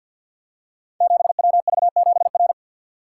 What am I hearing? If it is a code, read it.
Morse code, heard as a keyed tone, about 52 wpm, 700 Hz. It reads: 7WV7R
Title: Code Groups - 5 Letter:Numbers - 50wpm